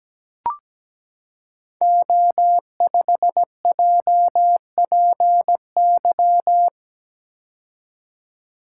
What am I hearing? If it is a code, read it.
Morse code, heard as a keyed tone, about 17 wpm, 700 Hz. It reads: O5JPY
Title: Code Groups - 5 Letter:Numbers - 17wpm